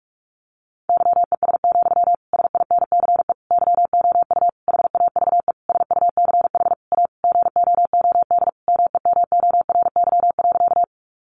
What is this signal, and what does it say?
Morse code, heard as a keyed tone, about 46 wpm, 700 Hz. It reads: YES, SINCE YOU HAVE SUCH A GOOD MEMORY.
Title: Sentences from Top 1300 Words 45wpm